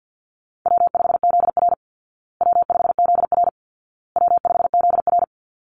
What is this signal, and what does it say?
Morse code, heard as a keyed tone, about 49 wpm, 700 Hz. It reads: W5ZR W5ZR W5ZR
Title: Easy Call Signs Repeated 3x - 50wpm